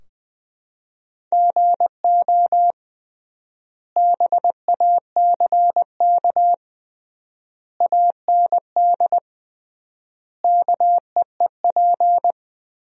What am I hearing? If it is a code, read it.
Morse code, heard as a keyed tone, about 20 wpm, 700 Hz. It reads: GO BACK AND KEEP
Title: 3x Word-Spacing - Sentences from Top 300 Words - 20wpm